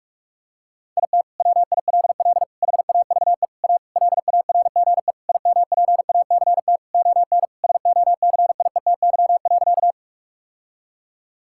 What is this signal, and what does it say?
Morse code, heard as a keyed tone, about 42 wpm, 700 Hz. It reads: IT WILL HAVE A LARGE IMPACT ON SOCIETY.
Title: Sentences from Top 1000 Words - Review Part 6 - 45wpm